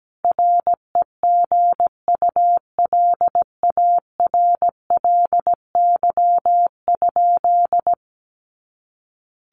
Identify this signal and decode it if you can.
Morse code, heard as a keyed tone, about 17 wpm, 700 Hz. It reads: REGULARLY?
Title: Sentences from Top 1400 Words 17wpm